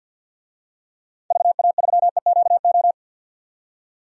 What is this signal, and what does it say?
Morse code, heard as a keyed tone, about 50 wpm, 700 Hz. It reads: VA3EQO
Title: DXCC-100 Call Signs - 6 Characters - 50wpm